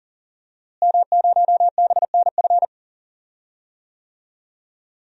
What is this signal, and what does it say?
Morse code, heard as a keyed tone, about 37 wpm, 700 Hz. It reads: M0BNF
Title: DXCC-100 Call Signs - 5 Characters - Repeated 2x - 40wpm